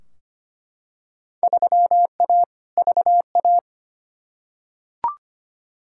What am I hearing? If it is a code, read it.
Morse code, heard as a keyed tone, about 25 wpm, 700 Hz. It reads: 3A VA
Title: Field Day Exchanges - 25wpm